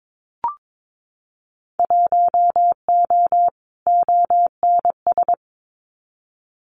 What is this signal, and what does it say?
Morse code, heard as a keyed tone, about 22 wpm, 700 Hz. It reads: 1O ONS
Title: Winter Field Day Exchanges Encore 22wpm